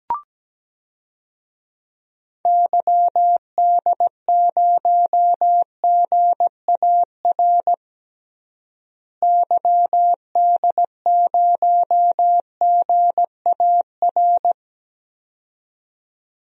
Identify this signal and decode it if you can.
Morse code, heard as a keyed tone, about 17 wpm, 700 Hz. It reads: YD0GAR YD0GAR
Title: DXCC-100 Call Signs - 6 Characters - Repeated 2x - 17wpm